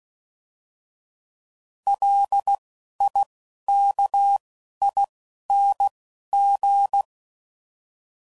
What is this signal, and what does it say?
Morse code, heard as a keyed tone, about 16 wpm, spaced as for 11 wpm, 800 Hz. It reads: LIKING